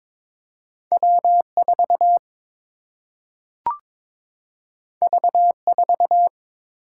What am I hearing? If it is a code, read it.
Morse code, heard as a keyed tone, about 22 wpm, 700 Hz. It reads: W 4 V 4
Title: Single-Letter Single-Number 22wpm